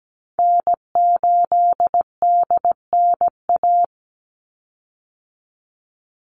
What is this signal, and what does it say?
Morse code, heard as a keyed tone, about 17 wpm, 700 Hz. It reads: N8DNA
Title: Call Signs Repeated 3x - 17wpm